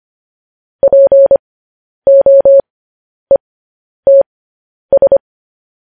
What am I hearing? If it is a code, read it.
Morse code, heard as a keyed tone, about 25 wpm, spaced as for 10 wpm, 550 Hz. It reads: POETS
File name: podcast / News headlines-1x-25x10